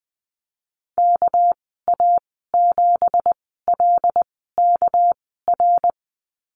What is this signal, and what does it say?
Morse code, heard as a keyed tone, about 19 wpm, spaced as for 14 wpm, 700 Hz. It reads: KA7LKR
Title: 238 Lesson 38g - Letter q - Call Signs 20wpm